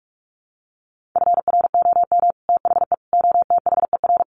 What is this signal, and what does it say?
Morse code, heard as a keyed tone, about 44 wpm, 700 Hz. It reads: FROM THE OTHER
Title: Sets of 5 Words Encore 45wpm